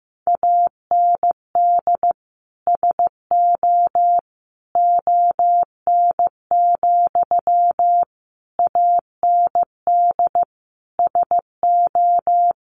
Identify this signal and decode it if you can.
Morse code, heard as a keyed tone, about 15 wpm, 700 Hz. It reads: AND SO ON, AND SO
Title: Sentences from Top 100 Words 15wpm